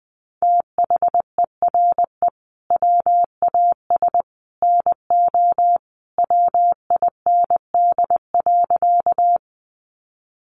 Morse code, heard as a keyed tone, about 20 wpm, 700 Hz. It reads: THERE WAS NO WIND.
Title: Sentences from Top 400 Words 20wpm